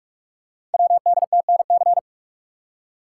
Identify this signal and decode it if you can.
Morse code, heard as a keyed tone, about 44 wpm, 700 Hz. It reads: WDTNC
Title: Code Groups - 5 Letters - 45wpm